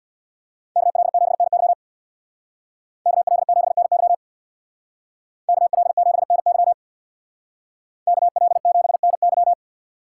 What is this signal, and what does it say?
Morse code, heard as a keyed tone, about 62 wpm, 700 Hz. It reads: KL7NY KL7NY KL7NY KL7NY
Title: Speed Racing - USA Call Signs 50wpm